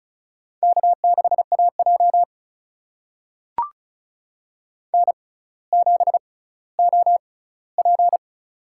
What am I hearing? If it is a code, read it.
Morse code, heard as a keyed tone, about 36 wpm, spaced as for 18 wpm, 700 Hz. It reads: K6AJ N7OP
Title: USA Call Signs - 4 Characters - 6x Character Spacing - 35wpm